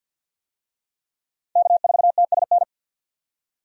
Morse code, heard as a keyed tone, about 48 wpm, 700 Hz. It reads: K4TSN
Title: USA Call Signs - 5 Characters - Repeated 2x - 50wpm